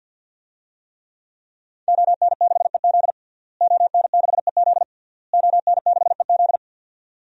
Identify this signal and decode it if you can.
Morse code, heard as a keyed tone, about 52 wpm, 700 Hz. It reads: ON6EZ ON6EZ ON6EZ
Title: DXCC-100 Call Signs - 5 Characters - Repeated 3x - 50wpm